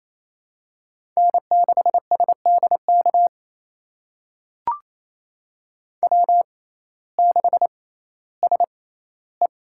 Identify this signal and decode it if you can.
Morse code, heard as a keyed tone, about 28 wpm, spaced as for 15 wpm, 700 Hz. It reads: N6SDK W6SE
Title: USA Call Signs - 6x Character Spacing - 28wpm